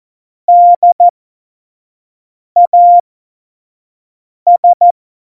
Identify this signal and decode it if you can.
Morse code, heard as a keyed tone, about 14 wpm, spaced as for 5 wpm, 700 Hz. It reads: DAS
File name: 25 F70